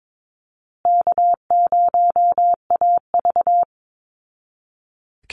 Morse code, heard as a keyed tone, about 22 wpm, 700 Hz. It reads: K0AV
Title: Call Signs Repeated 2x - 22wpm